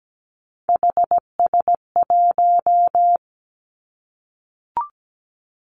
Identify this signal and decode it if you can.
Morse code, heard as a keyed tone, about 17 wpm, 700 Hz. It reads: HS1
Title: Three Challenging Letter-Number Combinations 17wpm